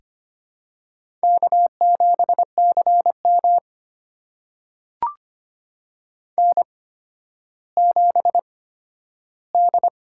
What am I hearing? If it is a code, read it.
Morse code, heard as a keyed tone, about 25 wpm, spaced as for 12 wpm, 700 Hz. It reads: K7CM N7D
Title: USA Call Signs - 4 Characters - 8x Character Spacing - 25wpm